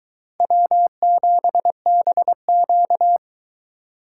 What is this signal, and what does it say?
Morse code, heard as a keyed tone, about 22 wpm, 700 Hz. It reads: W7BQ